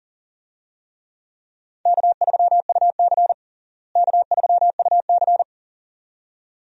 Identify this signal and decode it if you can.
Morse code, heard as a keyed tone, about 37 wpm, 700 Hz. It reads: K3UC K3UC
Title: USA Call Signs - 4 Characters - Repeated 2x - 40wpm